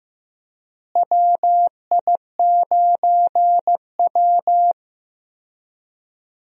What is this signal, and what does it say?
Morse code, heard as a keyed tone, about 15 wpm, 700 Hz. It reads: WI9W
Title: USA Call Signs - 4 Characters - Repeated 3x - 15wpm